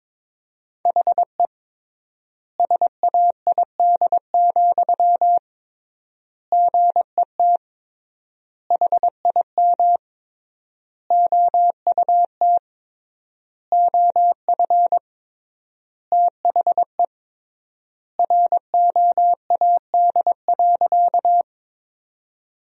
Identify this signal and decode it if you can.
Morse code, heard as a keyed tone, about 22 wpm, 700 Hz. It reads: HE SAID, GET HIM OUT OF THE ROAD.
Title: 3x Word-Spacing - Sentences from Top 400 Words - 22wpm